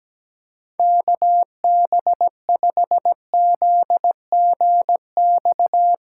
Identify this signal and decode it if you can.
Morse code, heard as a keyed tone, about 17 wpm, 700 Hz. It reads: KB5ZGX
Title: USA Callsign DE Callsign - 6 Characters - 17wpm